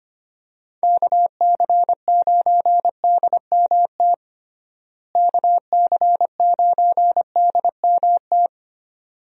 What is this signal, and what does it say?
Morse code, heard as a keyed tone, about 25 wpm, 700 Hz. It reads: KC9DMT KC9DMT
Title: USA Call Signs - 6 Characters - Repeated 2x - 25wpm